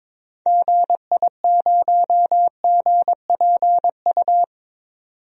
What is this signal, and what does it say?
Morse code, heard as a keyed tone, about 22 wpm, 700 Hz. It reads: GI0GPU
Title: DXCC-100 Call Signs - 6 Characters - 22wpm